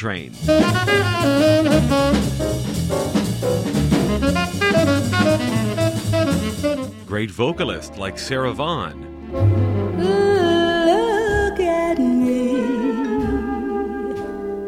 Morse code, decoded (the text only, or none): none